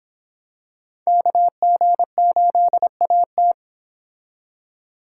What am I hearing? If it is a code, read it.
Morse code, heard as a keyed tone, about 25 wpm, 700 Hz. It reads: KG8AT